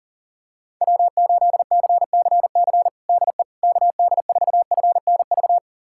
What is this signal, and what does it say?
Morse code, heard as a keyed tone, about 40 wpm, 700 Hz. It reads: W8CCC DE KD4FNV
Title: USA Callsign DE Callsign 40wpm